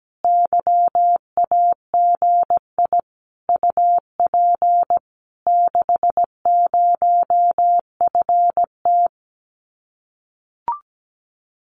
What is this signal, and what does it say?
Morse code, heard as a keyed tone, about 17 wpm, 700 Hz. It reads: YAGI UP 60FT
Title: Common QSO Phrases Encore 17wpm